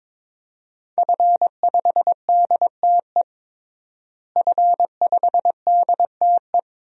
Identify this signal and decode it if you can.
Morse code, heard as a keyed tone, about 22 wpm, 700 Hz. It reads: F5DTE F5DTE
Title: DXCC-100 Call Signs - 5 Characters - Repeated 2x - 22wpm